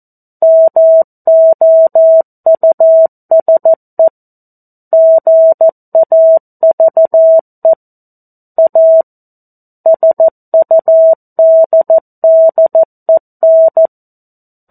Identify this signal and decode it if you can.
Morse code, heard as a keyed tone, about 14 wpm, 650 Hz. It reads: MOUSE GAVE A SUDDEN
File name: 02-Chapter-02 14 WPM